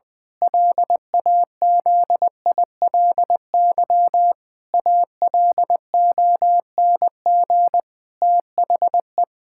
Morse code, heard as a keyed tone, about 20 wpm, 700 Hz. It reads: LAZILY ALONG THE
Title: Pangram Challenge v2 - 20wpm